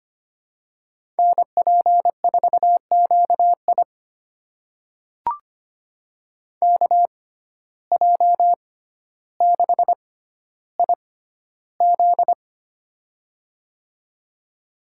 Morse code, heard as a keyed tone, about 25 wpm, spaced as for 13 wpm, 700 Hz. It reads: NP4QI KJ6IZ
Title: USA Call Signs - 5 Characters - 6x Character Spacing - 25wpm